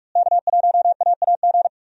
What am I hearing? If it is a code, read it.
Morse code, heard as a keyed tone, about 46 wpm, 700 Hz. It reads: K1AAG
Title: USA Call Signs - 5 Characters - Repeated 3x - 45wpm